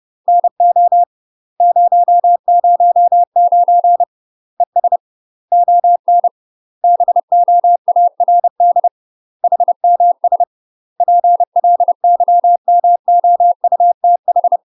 Morse code, heard as a keyed tone, about 30 wpm, 700 Hz. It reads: NO 009 ES ON BOARD HMS PLYMOUTH
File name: UK MASTER QSOs TRAINING / 30WPM / UK MASTER QSOs 0021